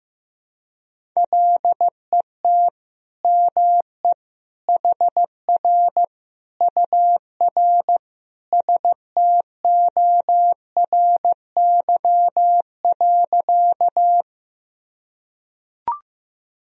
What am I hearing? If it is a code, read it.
Morse code, heard as a keyed tone, about 15 wpm, 700 Hz. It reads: LET ME HR UR STORY.
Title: Sentences from Top 300 Words and Abbreviations 15wpm